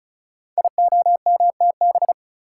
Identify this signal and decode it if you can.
Morse code, heard as a keyed tone, about 36 wpm, 700 Hz. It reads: IOMTB
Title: Code Groups - 5 Letters - 35wpm